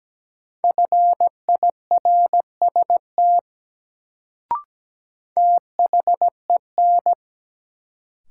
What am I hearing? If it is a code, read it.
Morse code, heard as a keyed tone, about 17 wpm, 700 Hz. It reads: FIRST THEN
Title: Sentences from Top 100 Words - New Word Drill - 17wpm